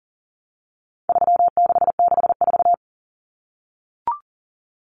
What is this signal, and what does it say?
Morse code, heard as a keyed tone, about 40 wpm, 700 Hz. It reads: 3664
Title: Common Cut Numbers - 4-Digit 40wpm